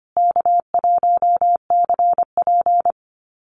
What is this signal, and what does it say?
Morse code, heard as a keyed tone, about 25 wpm, 700 Hz. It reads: K1CP
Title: Call Signs Repeated 2x - 25wpm